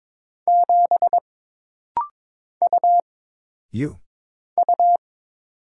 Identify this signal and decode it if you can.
Morse code, heard as a keyed tone, about 22 wpm, 700 Hz. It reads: 7 U U
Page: This rapid-fire timing is geared to push your ICR (Instant Character Recognition) skills.